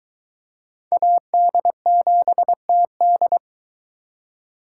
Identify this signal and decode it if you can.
Morse code, heard as a keyed tone, about 22 wpm, 700 Hz. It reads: AD7TD